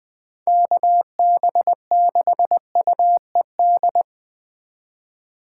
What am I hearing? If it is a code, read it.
Morse code, heard as a keyed tone, about 20 wpm, 700 Hz. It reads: KB6UED
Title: Speed Racing - USA 2x3 Call Signs 17wpm